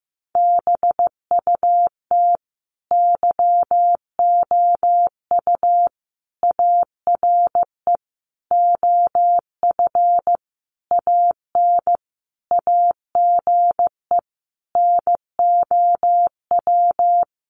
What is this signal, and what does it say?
Morse code, heard as a keyed tone, about 15 wpm, 700 Hz. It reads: BUT YOU ARE OF AN AGE NOW
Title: Sentences from Top 1300 Words 15wpm